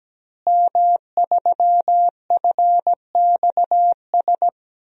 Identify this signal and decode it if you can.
Morse code, heard as a keyed tone, about 17 wpm, 700 Hz. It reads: M3FXS
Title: DXCC-100 Call Signs - 5 Characters - Repeated 2x - 17wpm